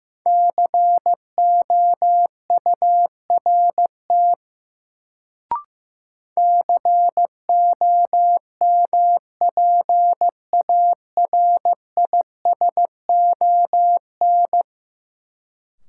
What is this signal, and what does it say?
Morse code, heard as a keyed tone, about 15 wpm, 700 Hz. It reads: COURT COMPARISON
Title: Sentences from Top 1600 Words - New Word Drill - 15wpm